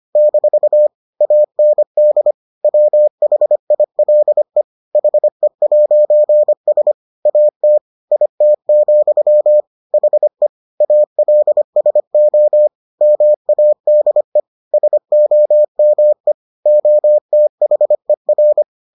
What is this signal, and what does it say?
Morse code, heard as a keyed tone, about 25 wpm, 600 Hz. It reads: - AND WHILE HE'S AT IT, HE ALSO MADE SOME OTHER